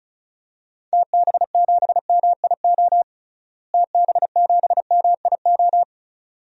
Morse code, heard as a keyed tone, about 36 wpm, 700 Hz. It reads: TB7MIO TB7MIO
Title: DXCC-100 Call Signs - 6 Characters - Repeated 2x - 35wpm